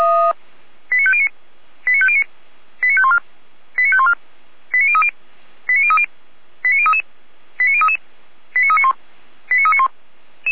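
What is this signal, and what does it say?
Five-tone selective call (ZVEI), heard as quick runs of five tones, tones of about 70 ms. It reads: 89509 89509 89514 89514 89030 89030 89033 89033 89391 89391
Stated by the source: ZVEI five-tone selective call received on SDR.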